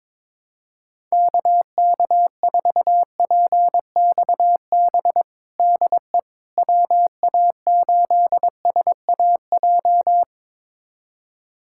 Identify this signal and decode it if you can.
Morse code, heard as a keyed tone, about 21 wpm, 700 Hz. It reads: KK4PXB DE WA8SAJ
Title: USA Callsign DE Callsign - 6 Characters - 22wpm